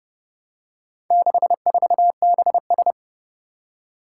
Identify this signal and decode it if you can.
Morse code, heard as a keyed tone, about 30 wpm, 700 Hz. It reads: 64BS
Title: Four Challenging Letter-Number Combinations 30wpm